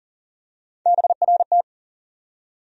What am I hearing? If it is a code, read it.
Morse code, heard as a keyed tone, about 40 wpm, 700 Hz. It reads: BRT
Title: USA Call Sign Postfixes - 3 Letters 40wpm